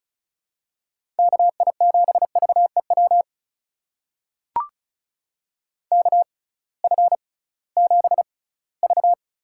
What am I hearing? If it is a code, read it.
Morse code, heard as a keyed tone, about 36 wpm, spaced as for 20 wpm, 700 Hz. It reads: KI7VEW KF7V
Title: USA Call Signs - 6 Characters - 6x Character Spacing - 35wpm